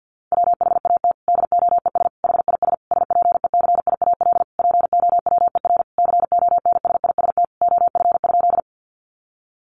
Morse code, heard as a keyed tone, about 50 wpm, 700 Hz. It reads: WHAT DOES HIS SPECIAL POWER CONSIST OF?
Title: Sentences from Top 1400 Words 50wpm